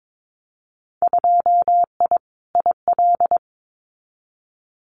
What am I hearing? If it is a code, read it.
Morse code, heard as a keyed tone, about 22 wpm, 700 Hz. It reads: 2I IL
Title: Winter Field Day Exchanges Encore 22wpm